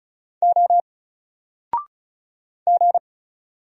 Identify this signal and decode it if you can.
Morse code, heard as a keyed tone, about 36 wpm, 700 Hz. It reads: O G
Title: Single Character (Letter:Number:Punctuation:Prosign) 35wpm